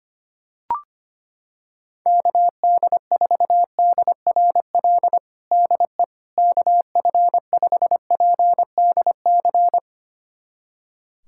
Morse code, heard as a keyed tone, about 25 wpm, 700 Hz. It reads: KD4DRL DE KF5PDC
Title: USA Callsign DE Callsign - 6 Characters - 25wpm